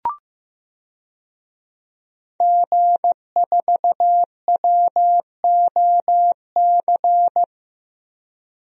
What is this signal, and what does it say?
Morse code, heard as a keyed tone, about 15 wpm, 700 Hz. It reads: G4WOC
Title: DXCC-100 Call Signs - 5 Characters - 15wpm